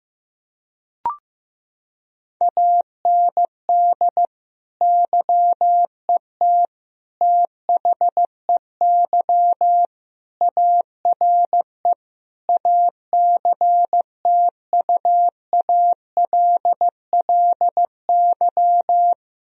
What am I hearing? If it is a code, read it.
Morse code, heard as a keyed tone, about 15 wpm, 700 Hz. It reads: AND YET THEY ARE ACTUALLY
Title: Sentences from Top 1000 Words - Review Part 4 - 15wpm